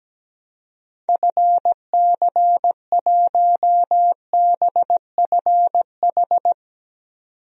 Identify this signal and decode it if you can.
Morse code, heard as a keyed tone, about 17 wpm, 700 Hz. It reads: FC1BFH
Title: DXCC-100 Call Signs - 6 Characters - 17wpm